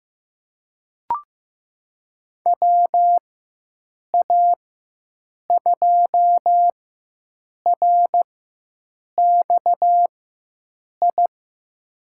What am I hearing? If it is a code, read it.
Morse code, heard as a keyed tone, about 15 wpm, spaced as for 7 wpm, 700 Hz. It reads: WA2RXI
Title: USA Call Signs - 6 Characters - 4x Character Spacing - 15wpm